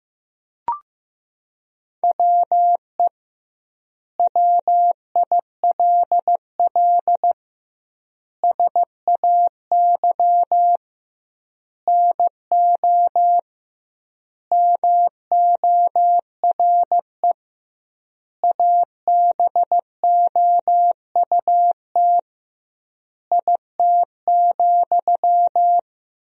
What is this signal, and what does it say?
Morse code, heard as a keyed tone, about 15 wpm, 700 Hz. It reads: WE WILL SAY NO MORE ABOUT IT,
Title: 2x Word-Spacing - Sentences from Top 200 Words - 15wpm